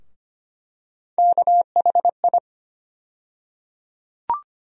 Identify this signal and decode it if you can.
Morse code, heard as a keyed tone, about 25 wpm, 700 Hz. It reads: KHI
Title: USA Call Sign Postfixes - 3 Letters 25wpm